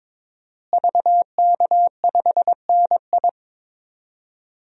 Morse code, heard as a keyed tone, about 21 wpm, 700 Hz. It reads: VK5NI
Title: DXCC-100 Call Signs - 5 Characters - Repeated 2x - 22wpm